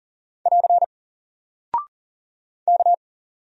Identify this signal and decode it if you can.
Morse code, heard as a keyed tone, about 40 wpm, 700 Hz. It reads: <AR> K
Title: Single Character (Letter:Number:Punctuation:Prosign) Alt 40wpm